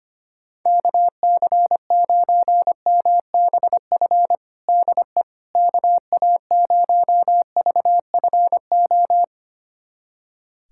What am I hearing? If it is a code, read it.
Morse code, heard as a keyed tone, about 25 wpm, 700 Hz. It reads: KC9MBF DE KA0VFO
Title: USA Callsign DE Callsign - 6 Characters - 25wpm